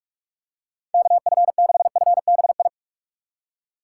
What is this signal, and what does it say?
Morse code, heard as a keyed tone, about 43 wpm, 700 Hz. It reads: KF6FBI
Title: USA Call Signs - 6 Characters - 4x Character Spacing - 45wpm